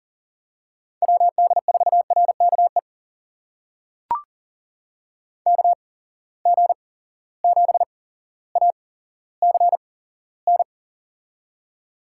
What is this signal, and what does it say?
Morse code, heard as a keyed tone, about 37 wpm, spaced as for 17 wpm, 700 Hz. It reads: WD4RKE KG7ACN
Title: USA Call Signs - 6 Characters - 8x Character Spacing - 40wpm